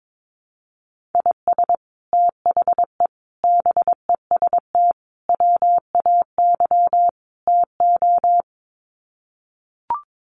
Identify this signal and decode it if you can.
Morse code, heard as a keyed tone, about 22 wpm, 700 Hz. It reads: IS THE BEST WAY TO
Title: Sets of 5 Words Encore 22wpm